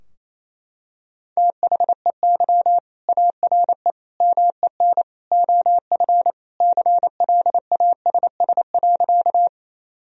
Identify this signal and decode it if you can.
Morse code, heard as a keyed tone, about 28 wpm, 700 Hz. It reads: THEY ARE MEN OF CLASS.
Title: Sentences from Top 1000 Words - Review Part 2 - 28wpm